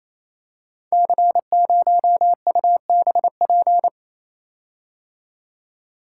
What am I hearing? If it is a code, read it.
Morse code, heard as a keyed tone, about 27 wpm, 700 Hz. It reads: C0UBP
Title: Code Groups - 5 Letter:Numbers - 28wpm